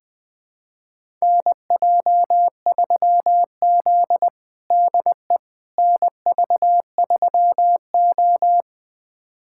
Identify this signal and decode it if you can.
Morse code, heard as a keyed tone, about 21 wpm, 700 Hz. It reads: NJ3Z DE NV3O
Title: USA Callsign DE Callsign - 4 Characters - 20wpm